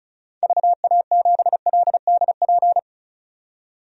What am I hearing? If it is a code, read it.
Morse code, heard as a keyed tone, about 34 wpm, 700 Hz. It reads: VA7LDP
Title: DXCC-100 Call Signs - 6 Characters - 35wpm